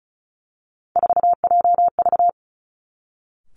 Morse code, heard as a keyed tone, about 36 wpm, 700 Hz. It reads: 4JV
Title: Three Challenging Letter-Number Combinations 35wpm